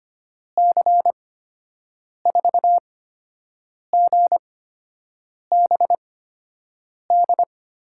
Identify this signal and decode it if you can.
Morse code, heard as a keyed tone, about 25 wpm, spaced as for 7 wpm, 700 Hz. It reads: C4GBD
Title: USA Call Signs - 6 Characters - 8x Character Spacing - 25wpm